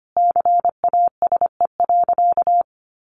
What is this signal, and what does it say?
Morse code, heard as a keyed tone, about 25 wpm, 700 Hz. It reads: CASE.
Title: Sentences from Top 1100 Words 25wpm